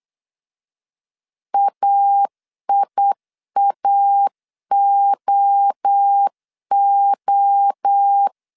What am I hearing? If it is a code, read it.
Morse code, heard as a keyed tone, about 8 wpm, 800 Hz. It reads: AIAOO
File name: CW / DET / 1 3 1500C S B 8wpm 4141